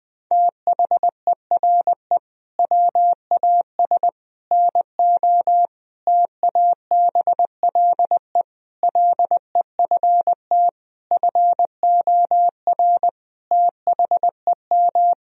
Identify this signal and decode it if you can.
Morse code, heard as a keyed tone, about 21 wpm, 700 Hz. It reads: THERE WAS NO TABLE LEFT FOR THEM
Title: Sentences from Top 1000 Words - Review Part 2 - 20wpm